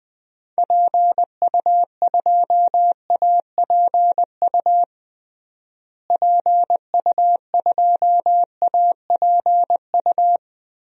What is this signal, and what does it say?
Morse code, heard as a keyed tone, about 21 wpm, 700 Hz. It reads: PU2APU PU2APU
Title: DXCC-100 Call Signs - 6 Characters - Repeated 2x - 20wpm